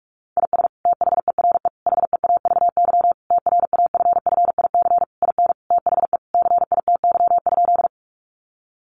Morse code, heard as a keyed tone, about 44 wpm, 700 Hz. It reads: IS THERE HEAVY TRAFFIC IN THE CITY?
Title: Sentences from Top 1300 Words 45wpm